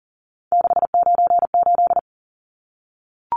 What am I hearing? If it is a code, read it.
Morse code, heard as a keyed tone, about 40 wpm, 700 Hz. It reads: 698
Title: Numbers - 3-Digit 40wpm